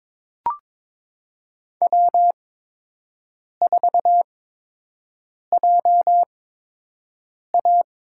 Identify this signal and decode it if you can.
Morse code, heard as a keyed tone, about 22 wpm, spaced as for 6 wpm, 700 Hz. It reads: W4JA
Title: USA Call Signs - 4 Characters - 8x Character Spacing - 22wpm